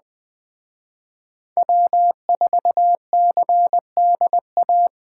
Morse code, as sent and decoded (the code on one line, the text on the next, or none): .-- ....- -.-. -.. .-
W4CDA